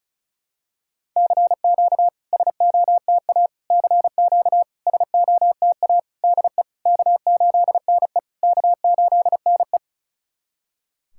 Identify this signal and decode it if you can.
Morse code, heard as a keyed tone, about 36 wpm, 690 Hz. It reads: CQ SOTA CQ SOTA DE K8NE K8NE
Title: SOTA QSO Practice 35wpm